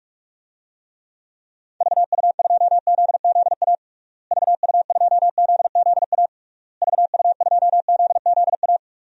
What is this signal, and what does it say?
Morse code, heard as a keyed tone, about 46 wpm, 700 Hz. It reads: VU2ZZA VU2ZZA VU2ZZA
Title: DXCC-100 Call Signs - 6 Characters - Repeated 3x - 45wpm